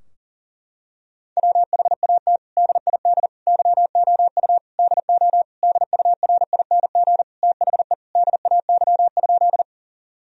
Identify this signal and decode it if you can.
Morse code, heard as a keyed tone, about 40 wpm, 700 Hz. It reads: WHAT DID YOU DO DURING THE DAY?
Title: Sentences from Top 1000 Words - Review Part 2 - 40wpm